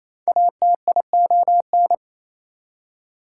Morse code, heard as a keyed tone, about 37 wpm, 690 Hz. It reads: ATION